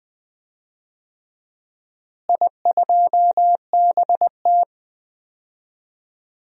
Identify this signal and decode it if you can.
Morse code, heard as a keyed tone, about 20 wpm, 700 Hz. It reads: I2BT
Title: DXCC-100 Call Signs - 4 Characters - Repeated 2x - 20wpm